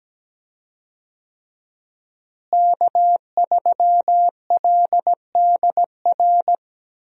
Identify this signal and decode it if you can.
Morse code, heard as a keyed tone, about 17 wpm, 700 Hz. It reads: K3LDR
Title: USA Call Signs - 5 Characters - Repeated 3x - 17wpm